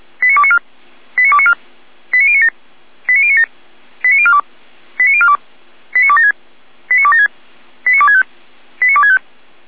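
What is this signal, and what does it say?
Five-tone selective call (ZVEI), heard as quick runs of five tones, tones of about 70 ms. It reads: 89294 89294 89097 89097 89042 89042 89276 89276 89265 89265
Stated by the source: ZVEI five-tone selective call received by SDR.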